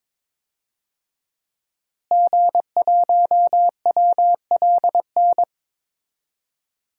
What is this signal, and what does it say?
Morse code, heard as a keyed tone, about 22 wpm, 700 Hz. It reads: G1WLN
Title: DXCC-100 Call Signs - 5 Characters - Repeated 3x - 22wpm